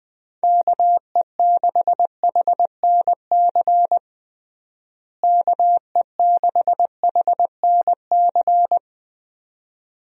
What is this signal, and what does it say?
Morse code, heard as a keyed tone, about 20 wpm, 700 Hz. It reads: KE6HNC KE6HNC
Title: USA Call Signs - 6 Characters - Repeated 2x - 20wpm